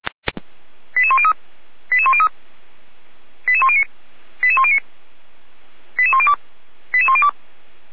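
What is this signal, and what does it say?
Five-tone selective call (ZVEI), heard as quick runs of five tones, tones of about 70 ms. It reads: 88103 88103 88109 88109 88102 88102
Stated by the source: ZVEI five-tone selective call 88074, received by SDR.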